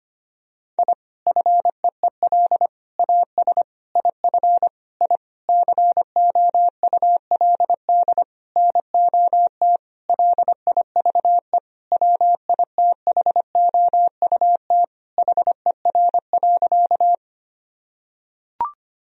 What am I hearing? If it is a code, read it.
Morse code, heard as a keyed tone, about 25 wpm, 700 Hz. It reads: I FEEL AS IF I COULD NOT LIVE WITHOUT HER.
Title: Sentences from Top 1000 Words - Review Part 3 - 25wpm